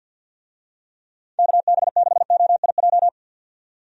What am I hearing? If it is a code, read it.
Morse code, heard as a keyed tone, about 51 wpm, 700 Hz. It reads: KB6OIJ
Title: USA Call Signs - 6 Characters - 50wpm